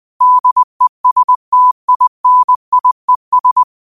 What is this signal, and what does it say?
Morse code, heard as a keyed tone, about 20 wpm, 1000 Hz. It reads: DESTINIES
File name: qotdfiles / 20wpm